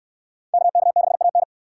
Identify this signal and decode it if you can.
Morse code, heard as a keyed tone, about 66 wpm, 690 Hz. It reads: KK6AN